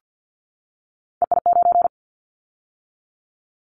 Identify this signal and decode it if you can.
Morse code, heard as a keyed tone, about 50 wpm, 700 Hz. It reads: EI9
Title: Callsign Prefixes for Top 50 DXCC - 50wpm